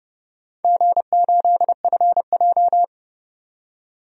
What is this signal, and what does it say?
Morse code, heard as a keyed tone, about 31 wpm, 700 Hz. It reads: G8FJ
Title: DXCC-100 Call Signs - 4 Characters - 30wpm